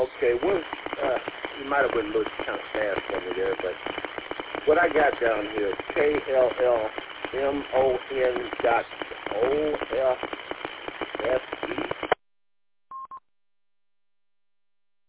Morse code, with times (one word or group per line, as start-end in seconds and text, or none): none